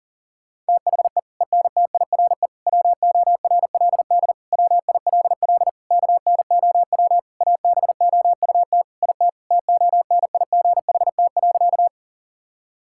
Morse code, heard as a keyed tone, about 40 wpm, 700 Hz. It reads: THE ENTIRE WORLD WILL KNOW ABOUT IT TONIGHT.
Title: Sentences from Top 1000 Words - Review Part 5 - 40wpm